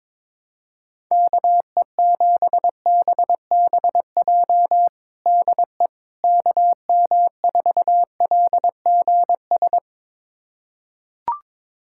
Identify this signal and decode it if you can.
Morse code, heard as a keyed tone, about 22 wpm, 700 Hz. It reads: KE7BBJ DE KM4LGS
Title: USA Callsign DE Callsign 22wpm